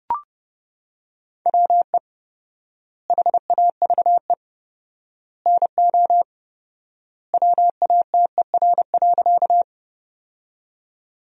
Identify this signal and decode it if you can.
Morse code, heard as a keyed tone, about 30 wpm, 700 Hz. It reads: WE HAVE NO WATER.
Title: 4x Word-Spacing - Sentences from Top 100 Words - 30wpm